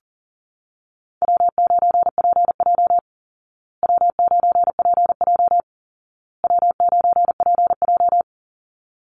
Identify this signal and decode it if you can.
Morse code, heard as a keyed tone, about 40 wpm, 700 Hz. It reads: W9PJ W9PJ W9PJ
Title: Easy Call Signs Repeated 3x - 40wpm